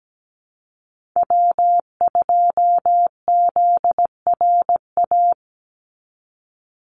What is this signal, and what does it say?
Morse code, heard as a keyed tone, about 17 wpm, 700 Hz. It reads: W2ZRA
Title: Call Signs Repeated 2x - 17wpm